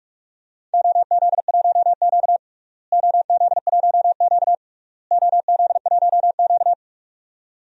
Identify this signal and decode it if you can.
Morse code, heard as a keyed tone, about 46 wpm, 700 Hz. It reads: OZ1Q OZ1Q OZ1Q
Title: DXCC-100 Call Signs - 4 Characters - Repeated 3x - 45wpm